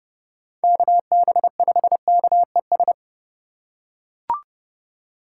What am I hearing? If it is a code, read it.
Morse code, heard as a keyed tone, about 29 wpm, 700 Hz. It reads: KB5KES